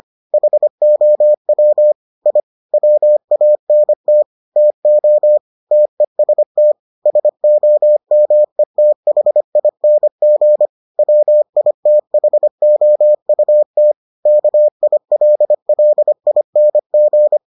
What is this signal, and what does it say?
Morse code, heard as a keyed tone, about 25 wpm, 600 Hz. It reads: HOW I WANT TO TEST SOMETHING WITHOUT KILLING